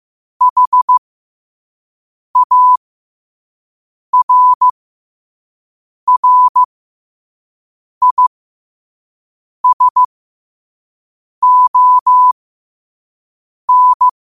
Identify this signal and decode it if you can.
Morse code, heard as a keyed tone, about 15 wpm, spaced as for 5 wpm, 1000 Hz. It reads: HARRISON
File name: qotdfiles / 5wpm